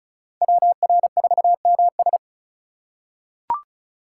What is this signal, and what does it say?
Morse code, heard as a keyed tone, about 36 wpm, 700 Hz. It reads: WR4MS
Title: USA Call Signs - 5 Characters - 4x Character Spacing - 35wpm